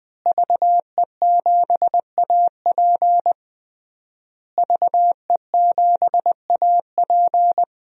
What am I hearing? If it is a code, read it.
Morse code, heard as a keyed tone, about 20 wpm, 700 Hz. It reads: VE7AP VE7AP
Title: DXCC-100 Call Signs - 5 Characters - Repeated 3x - 20wpm